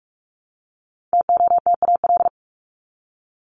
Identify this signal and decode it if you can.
Morse code, heard as a keyed tone, about 44 wpm, 700 Hz. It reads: TOTAL
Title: LY Suffix - Top 100 Words with Base Words 45wpm